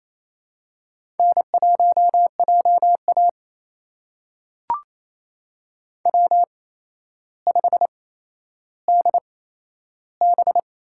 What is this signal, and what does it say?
Morse code, heard as a keyed tone, about 28 wpm, spaced as for 12 wpm, 700 Hz. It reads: N1JA W5DB
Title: USA Call Signs - 4 Characters - 8x Character Spacing - 28wpm